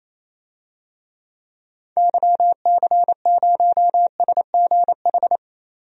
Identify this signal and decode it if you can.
Morse code, heard as a keyed tone, about 28 wpm, 700 Hz. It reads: YC0SGH
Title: DXCC-100 Call Signs - 6 Characters - Repeated 3x - 28wpm